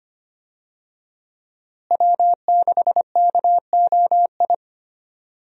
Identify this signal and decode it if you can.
Morse code, heard as a keyed tone, about 25 wpm, 700 Hz. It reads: W6KOI
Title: USA Call Signs - 5 Characters - Repeated 3x - 25wpm